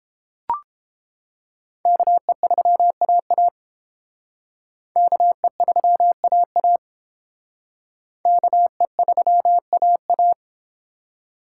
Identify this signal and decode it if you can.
Morse code, heard as a keyed tone, about 31 wpm, 700 Hz. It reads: KE3AA KE3AA KE3AA